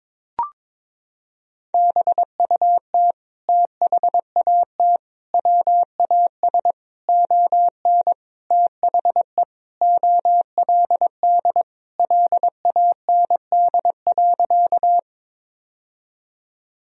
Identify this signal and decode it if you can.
Morse code, heard as a keyed tone, about 21 wpm, 700 Hz. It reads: BUT THAT WAS ON THE OLD LAND.
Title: Sentences from Top 1000 Words - Review Part 1 - 22wpm